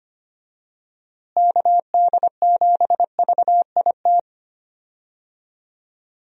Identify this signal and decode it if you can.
Morse code, heard as a keyed tone, about 25 wpm, 700 Hz. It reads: KD7VIT
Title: USA Call Signs - 6 Characters - Repeated 2x - 25wpm